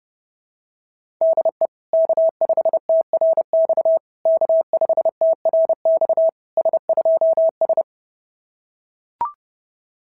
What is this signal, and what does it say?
Morse code, heard as a keyed tone, about 30 wpm, 660 Hz. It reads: DE K5TRX K5TRX S2S
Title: SOTA QSO Practice 30wpm